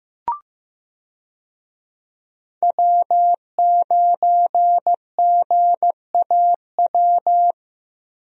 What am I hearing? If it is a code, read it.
Morse code, heard as a keyed tone, about 15 wpm, 700 Hz. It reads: W9GAW
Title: USA Call Signs - 5 Characters - Repeated 2x - 15wpm